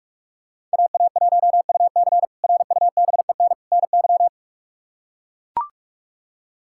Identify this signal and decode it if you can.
Morse code, heard as a keyed tone, about 41 wpm, 700 Hz. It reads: AA1UC RUBEN NY
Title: K1USN SST QSO Practice - 45wpm